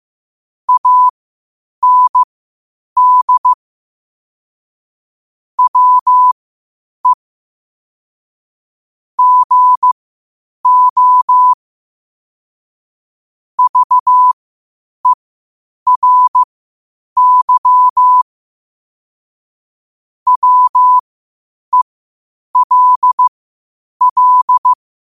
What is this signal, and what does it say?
Morse code, heard as a keyed tone, about 15 wpm, spaced as for 8 wpm, 1000 Hz. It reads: AND WE GO VERY WELL
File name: qotdfiles / 8wpm